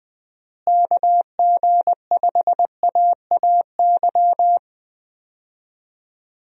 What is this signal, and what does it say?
Morse code, heard as a keyed tone, about 20 wpm, 700 Hz. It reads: KG5AAY
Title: USA Call Signs - 6 Characters - Repeated 3x - 20wpm